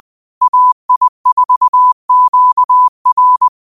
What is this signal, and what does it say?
Morse code, heard as a keyed tone, about 20 wpm, 1000 Hz. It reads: AI4QR
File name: qotdfiles / 20wpm